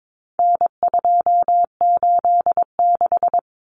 Morse code, heard as a keyed tone, about 22 wpm, 700 Hz. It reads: N286
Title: Common Cut Numbers - 4-Digit 22wpm